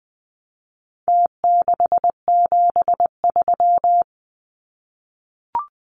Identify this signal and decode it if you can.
Morse code, heard as a keyed tone, about 20 wpm, 700 Hz. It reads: T673
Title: Numbers - Contest Style - 4-Digit 20wpm